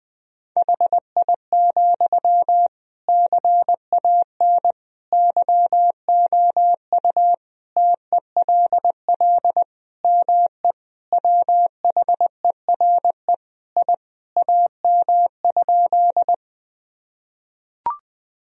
Morse code, heard as a keyed tone, about 21 wpm, 700 Hz. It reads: HI, CAN YOU TELL ME WHERE I AM?
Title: Sentences from Top 1000 Words - Review Part 3 - 20wpm